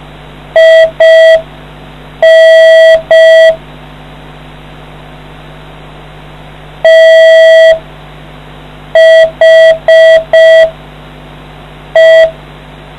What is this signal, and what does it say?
Morse code, heard as a keyed tone, about 5 wpm, spaced as for 4 wpm, 650 Hz. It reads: IN THE